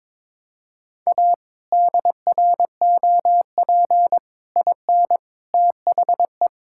Morse code, heard as a keyed tone, about 22 wpm, 700 Hz. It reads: A DROP IN THE
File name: practice / 22wpm / Sets of 8 Words - 22wpm